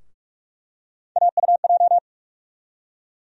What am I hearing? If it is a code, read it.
Morse code, heard as a keyed tone, about 47 wpm, 700 Hz. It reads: AUJ
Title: USA Call Sign Postfixes - 3 Letters 45wpm